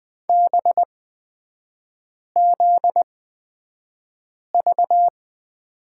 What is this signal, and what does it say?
Morse code, heard as a keyed tone, about 20 wpm, spaced as for 5 wpm, 700 Hz. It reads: BZV
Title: 243 Lesson 39e - Letter z - Three Characters 20wpm